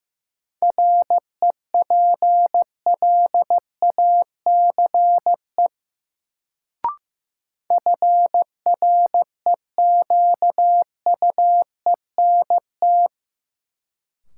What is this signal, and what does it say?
Morse code, heard as a keyed tone, about 15 wpm, 700 Hz. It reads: REPLACE FREQUENT
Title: Sentences from Top 1400 Words - New Word Drill - 15wpm